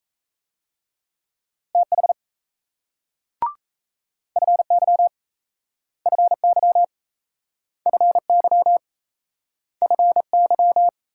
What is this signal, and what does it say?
Morse code, heard as a keyed tone, about 38 wpm, 700 Hz. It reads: TH FY FY FY FY